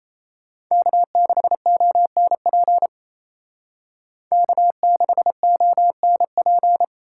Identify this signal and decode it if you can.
Morse code, heard as a keyed tone, about 36 wpm, 700 Hz. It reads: K6ONP K6ONP